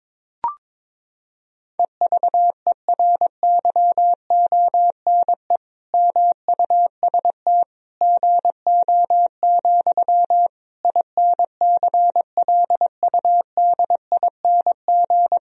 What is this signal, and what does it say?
Morse code, heard as a keyed tone, about 22 wpm, 700 Hz. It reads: EVERYONE MUST GO, INCLUDING
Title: Sentences from Top 1000 Words - Review Part 4 - 22wpm